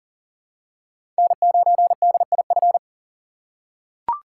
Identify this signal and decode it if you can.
Morse code, heard as a keyed tone, about 40 wpm, 700 Hz. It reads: N9DIF
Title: USA Call Signs - 6x Character Spacing - 40wpm